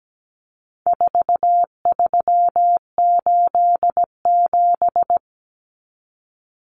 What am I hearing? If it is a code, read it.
Morse code, heard as a keyed tone, about 17 wpm, 700 Hz. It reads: 4387
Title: Numbers - 4-Digit 17wpm